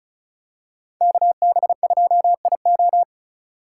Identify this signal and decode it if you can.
Morse code, heard as a keyed tone, about 36 wpm, 700 Hz. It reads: KB2IO
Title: USA Call Signs - 5 Characters - 4x Character Spacing - 35wpm